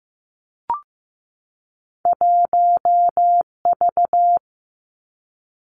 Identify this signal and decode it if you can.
Morse code, heard as a keyed tone, about 15 wpm, 700 Hz. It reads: 1V
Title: Two Challenging Letter-Number Combinations 15wpm